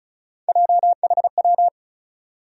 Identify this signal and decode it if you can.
Morse code, heard as a keyed tone, about 36 wpm, 700 Hz. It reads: JHW
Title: USA Call Sign Postfixes - 3 Letters 35wpm